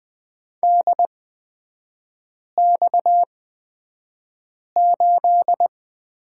Morse code, heard as a keyed tone, about 20 wpm, spaced as for 5 wpm, 690 Hz. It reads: DX8
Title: 229 Lesson 37e - Letter x - Three Characters 20wpm